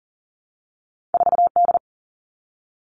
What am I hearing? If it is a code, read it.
Morse code, heard as a keyed tone, about 42 wpm, 700 Hz. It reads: 4D
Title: All Cut Numbers - 2-Digit 40wpm